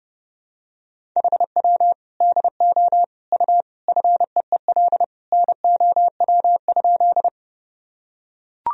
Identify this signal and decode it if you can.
Morse code, heard as a keyed tone, about 30 wpm, 700 Hz. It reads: HW DO U FEEL NOW?
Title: Sentences from Top 400 Words and Abbreviations 30wpm